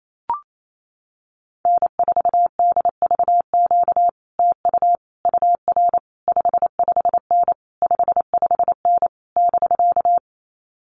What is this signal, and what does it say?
Morse code, heard as a keyed tone, about 28 wpm, 700 Hz. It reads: N4DVQ TU UR 55N 55N <BK>
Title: Common QSO Phrases Encore 28wpm